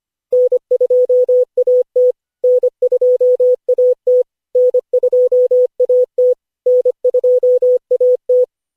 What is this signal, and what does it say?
Morse code, heard as a keyed tone, about 25 wpm, 500 Hz. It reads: N2AT N2AT N2AT N2AT
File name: cw / Session5 cw training